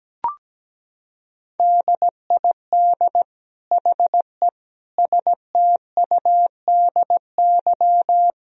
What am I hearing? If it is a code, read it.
Morse code, heard as a keyed tone, about 17 wpm, 700 Hz. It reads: DID HE STUDY
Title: Sentences from Top 1000 Words - Review Part 1 - 17wpm